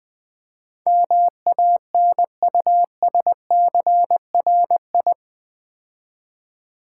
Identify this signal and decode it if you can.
Morse code, heard as a keyed tone, about 20 wpm, 700 Hz. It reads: MANUSCRI
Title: Build A Word - 4 Letters At A Time - 10-Letter Words - 20wpm